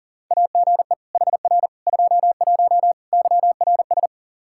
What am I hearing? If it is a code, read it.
Morse code, heard as a keyed tone, about 37 wpm, 700 Hz. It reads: AGE HR 21 YRS
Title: K1USN SST QSO Practice 40wpm